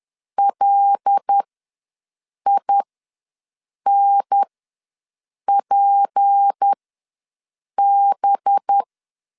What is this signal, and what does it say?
Morse code, heard as a keyed tone, about 10 wpm, spaced as for 6 wpm, 800 Hz. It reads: LINPB
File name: LEZIONI / 6 PKX0 / 6 PKX ws 10wpm 3706